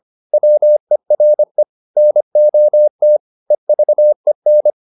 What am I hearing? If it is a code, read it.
Morse code, heard as a keyed tone, about 25 wpm, 600 Hz. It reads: WERE NOT EVEN